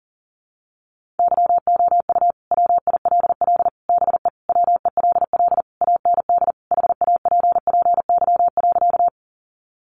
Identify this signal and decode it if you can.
Morse code, heard as a keyed tone, about 40 wpm, 700 Hz. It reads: YOU WILL BE WELL AND HAPPY.
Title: Sentences from Top 1000 Words 40wpm